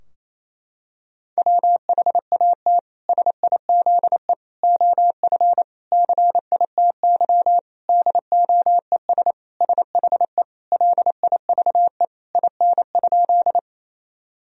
Morse code, heard as a keyed tone, about 28 wpm, 700 Hz. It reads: WHAT SIZE OF CITY DOES SHE LIVE IN?
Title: Sentences from Top 1000 Words - Review Part 2 - 28wpm